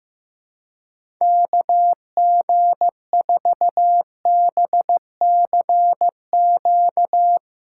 Morse code, heard as a keyed tone, about 15 wpm, 700 Hz. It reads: KG4BCQ
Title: USA Callsign DE Callsign - 6 Characters - 15wpm